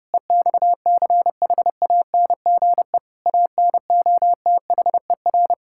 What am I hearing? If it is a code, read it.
Morse code, heard as a keyed tone, about 30 wpm, 700 Hz. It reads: EXCHANGE ANOTHER
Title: Sentences from Top 1000 Words - Review Part 6 - 30wpm